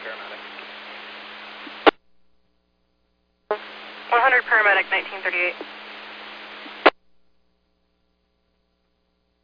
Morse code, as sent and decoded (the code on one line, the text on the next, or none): none